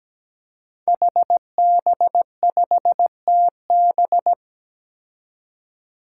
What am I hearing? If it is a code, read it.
Morse code, heard as a keyed tone, about 17 wpm, 700 Hz. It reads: HB5TB
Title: Five Challenging Letter-Number Combinations 17wpm